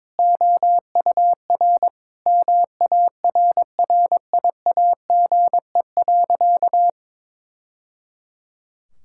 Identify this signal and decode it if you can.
Morse code, heard as a keyed tone, about 22 wpm, 700 Hz. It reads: OUR MARRIAGE.
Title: Sentences from Top 1000 Words - Review Part 5 - 22wpm